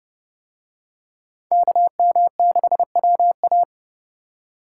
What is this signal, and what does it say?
Morse code, heard as a keyed tone, about 28 wpm, 700 Hz. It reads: KM6WA